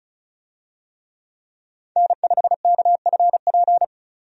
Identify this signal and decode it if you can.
Morse code, heard as a keyed tone, about 36 wpm, 700 Hz. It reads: N5KFP
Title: USA Call Signs - 5 Characters - 35wpm